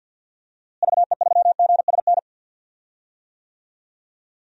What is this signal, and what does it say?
Morse code, heard as a keyed tone, about 49 wpm, 700 Hz. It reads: VE3GSN
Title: DXCC-100 Call Signs - 6 Characters - Repeated 2x - 50wpm